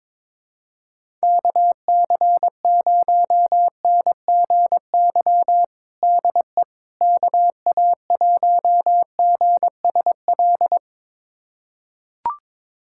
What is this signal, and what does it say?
Morse code, heard as a keyed tone, about 22 wpm, 700 Hz. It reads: KC0NGY DE KA1GSL
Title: USA Callsign DE Callsign - 6 Characters - 22wpm